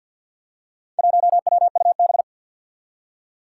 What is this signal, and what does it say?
Morse code, heard as a keyed tone, about 49 wpm, 700 Hz. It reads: 1WUB
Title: Four Challenging Letter-Number Combinations 50wpm